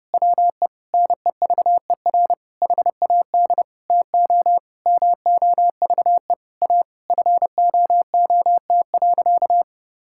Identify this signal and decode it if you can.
Morse code, heard as a keyed tone, about 30 wpm, 700 Hz. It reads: WE NEVER HAD TO MOVE A FOOT.
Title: Sentences from Top 1600 Words 30wpm